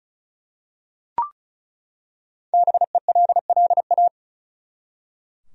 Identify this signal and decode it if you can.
Morse code, heard as a keyed tone, about 36 wpm, 700 Hz. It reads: BELLA
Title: USA Common Names - Max 5 Letters - 35wpm